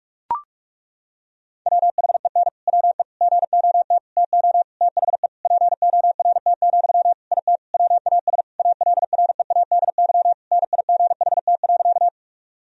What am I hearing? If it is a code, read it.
Morse code, heard as a keyed tone, about 41 wpm, 700 Hz. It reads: WHEN WE GOT TO THE PORT, IT WAS ALREADY NIGHT.
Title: Sentences from Top 1000 Words - Review Part 1 - 45wpm